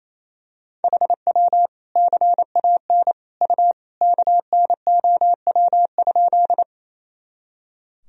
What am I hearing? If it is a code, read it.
Morse code, heard as a keyed tone, about 28 wpm, 700 Hz. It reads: HW CAN U KNOW?
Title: Sentences from Top 100 Words and Abbreviations 28wpm